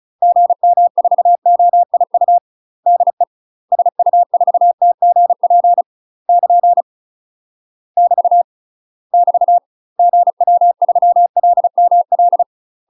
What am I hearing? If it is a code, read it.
Morse code, heard as a keyed tone, about 34 wpm, 700 Hz. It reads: GM4OIU DE SU4TGP <KN> <BT> <BT> GW3LML
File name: UK MASTER QSOs TRAINING / 35WPM / UK MASTER QSOs 0029